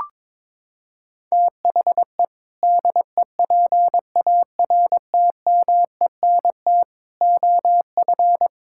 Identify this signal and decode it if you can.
Morse code, heard as a keyed tone, about 21 wpm, 700 Hz. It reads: THE DEPARTMENT OF
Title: Sentences from Top 1000 Words - Review Part 5 - 22wpm